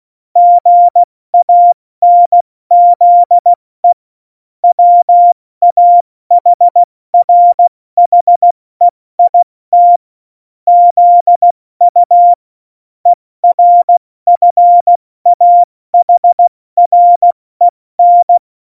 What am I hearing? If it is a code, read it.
Morse code, heard as a keyed tone, about 16 wpm, 700 Hz. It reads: GANZE WAHRHEIT ZU ERFAHREN
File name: Morsekurs / Witze / Witz 12. September 17, Tempo 70